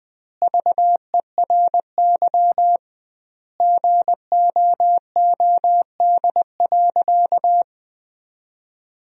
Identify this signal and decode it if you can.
Morse code, heard as a keyed tone, about 20 wpm, 700 Hz. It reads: VERY GOOD.
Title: 2x Word-Spacing - Sentences from Top 200 Words - 20wpm